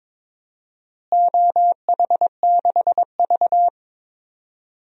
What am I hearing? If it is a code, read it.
Morse code, heard as a keyed tone, about 22 wpm, 700 Hz. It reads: OH6V
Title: DXCC-100 Call Signs - 4 Characters - Repeated 2x - 22wpm